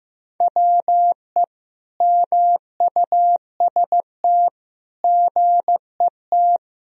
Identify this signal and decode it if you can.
Morse code, heard as a keyed tone, about 15 wpm, 700 Hz. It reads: WE MUST GET
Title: Sentences from Top 1000 Words - Review Part 2 - 15wpm